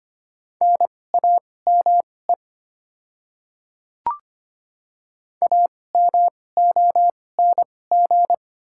Morse code, heard as a keyed tone, about 25 wpm, spaced as for 18 wpm, 700 Hz. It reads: NAME AMONG
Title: 2x Character Spacing - Top 500 Words - 25wpm